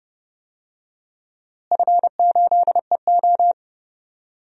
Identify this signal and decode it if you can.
Morse code, heard as a keyed tone, about 30 wpm, 700 Hz. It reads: F8EO
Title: DXCC-100 Call Signs - 4 Characters - Repeated 3x - 30wpm